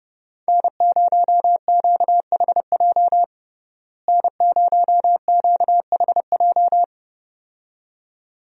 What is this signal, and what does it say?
Morse code, heard as a keyed tone, about 29 wpm, 700 Hz. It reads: N0QHJ N0QHJ
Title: USA Call Signs - 5 Characters - Repeated 2x - 30wpm